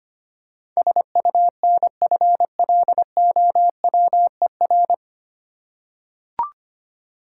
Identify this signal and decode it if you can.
Morse code, heard as a keyed tone, about 25 wpm, 700 Hz. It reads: SUNFLOWER
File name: practice / 25wpm / Difficult Words with Hints - 25wpm